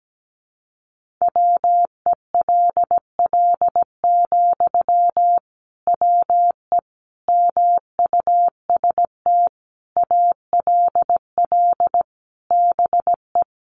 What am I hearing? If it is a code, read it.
Morse code, heard as a keyed tone, about 17 wpm, 700 Hz. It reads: WELL, WE MUST ALL BE
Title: Sentences from Top 1400 Words 17wpm